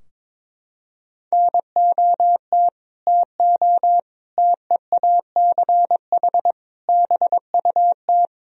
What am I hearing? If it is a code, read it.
Morse code, heard as a keyed tone, about 21 wpm, 700 Hz. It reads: NOT TO TEACH BUT
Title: Sentences from Top 1000 Words - Review Part 4 - 22wpm